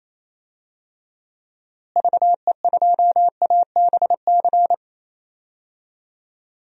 Morse code, heard as a keyed tone, about 27 wpm, 700 Hz. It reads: VE2ABC
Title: DXCC-100 Call Signs - 6 Characters - Repeated 3x - 28wpm